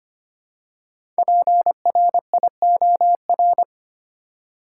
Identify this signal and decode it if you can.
Morse code, heard as a keyed tone, about 25 wpm, 700 Hz. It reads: PRIOR
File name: 5 Letter Words - First 3 Letters Given - 25wpm